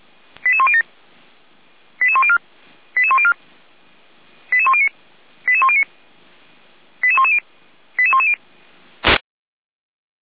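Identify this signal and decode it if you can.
Five-tone selective call (ZVEI), heard as quick runs of five tones, tones of about 70 ms. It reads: 88107 88104 88104 88109 88109 88110 88110
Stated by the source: ZVEI five-tone selective call received on an SDR, logged by Feuerwehr Orlamünde fire brigade.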